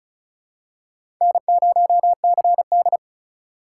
Morse code, heard as a keyed tone, about 36 wpm, 700 Hz. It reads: N0CD
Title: USA Call Signs - 6x Character Spacing - 35wpm